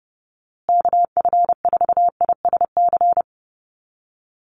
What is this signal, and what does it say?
Morse code, heard as a keyed tone, about 30 wpm, 700 Hz. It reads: KF4ISC
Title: USA Call Signs Repeated 3x 30wpm